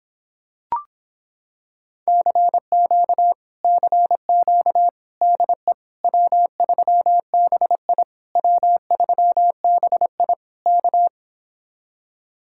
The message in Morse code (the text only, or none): CQ CQ DE W3BI W3BI K